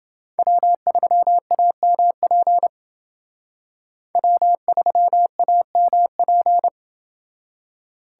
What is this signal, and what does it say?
Morse code, heard as a keyed tone, about 28 wpm, 700 Hz. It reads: W3AMP W3AMP